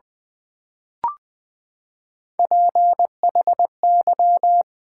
Morse code, heard as a keyed tone, about 20 wpm, 700 Hz. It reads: PHY
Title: Build A Word - 3 Letters At A Time - 9-Letter Words - 20wpm